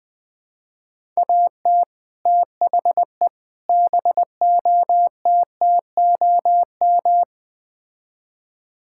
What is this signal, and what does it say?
Morse code, heard as a keyed tone, about 20 wpm, 700 Hz. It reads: AT THE BOTTOM
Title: Build A Phrase - 4-Word Phrases - 20wpm